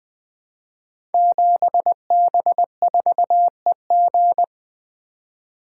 Morse code, heard as a keyed tone, about 20 wpm, 700 Hz. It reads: 7B4EG
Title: Code Groups - 5 Letter:Numbers - 20wpm